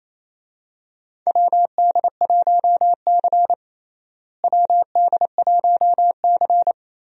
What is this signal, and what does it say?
Morse code, heard as a keyed tone, about 28 wpm, 700 Hz. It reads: WD1C WD1C
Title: USA Call Signs - 4 Characters - Repeated 2x - 28wpm